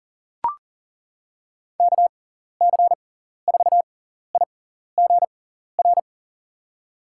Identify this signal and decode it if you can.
Morse code, heard as a keyed tone, about 37 wpm, spaced as for 14 wpm, 700 Hz. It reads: KC4IGR
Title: USA Call Signs - 6 Characters - 6x Character Spacing - 40wpm